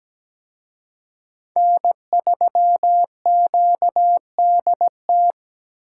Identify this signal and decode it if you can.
Morse code, heard as a keyed tone, about 20 wpm, 700 Hz. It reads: N3QDT
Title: Speed Racing - USA 1x3 Call Signs 17wpm